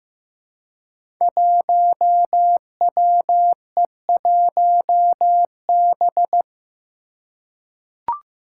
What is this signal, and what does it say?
Morse code, heard as a keyed tone, about 15 wpm, 700 Hz. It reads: 1WE1B
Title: Five Challenging Letter-Number Combinations 15wpm